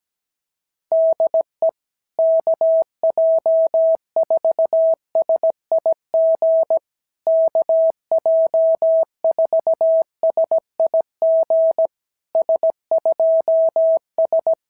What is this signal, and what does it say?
Morse code, heard as a keyed tone, about 16 wpm, 650 Hz. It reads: DE KJ4SIG KJ4SIG S2S
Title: SOTA QSO Practice 17wpm